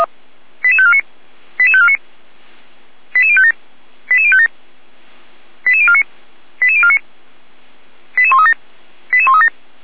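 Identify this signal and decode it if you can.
Five-tone selective call (ZVEI), heard as quick runs of five tones, tones of about 70 ms. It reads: 88549 88549 88057 88057 88049 88049 88137 88137
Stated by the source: ZVEI five-tone selective call received on an SDR.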